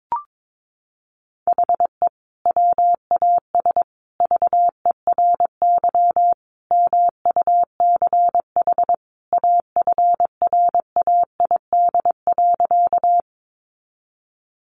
Morse code, heard as a keyed tone, about 22 wpm, 700 Hz. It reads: HE WAS VERY MUCH AFRAID.
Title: Sentences from Top 1500 Words 22wpm